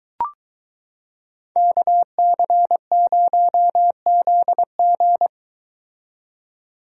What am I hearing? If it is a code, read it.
Morse code, heard as a keyed tone, about 22 wpm, 700 Hz. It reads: KC0ZG